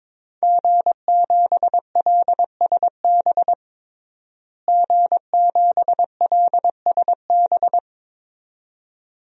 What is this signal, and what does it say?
Morse code, heard as a keyed tone, about 22 wpm, 700 Hz. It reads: G7LSB G7LSB
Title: DXCC-100 Call Signs - 5 Characters - Repeated 2x - 22wpm